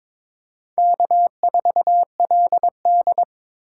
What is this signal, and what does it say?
Morse code, heard as a keyed tone, about 22 wpm, 700 Hz. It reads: K4LD
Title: USA Call Signs - 4x Character Spacing - 22wpm